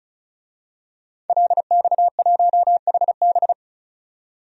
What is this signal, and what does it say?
Morse code, heard as a keyed tone, about 34 wpm, 700 Hz. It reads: LX1HB
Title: DXCC-100 Call Signs - 5 Characters - 35wpm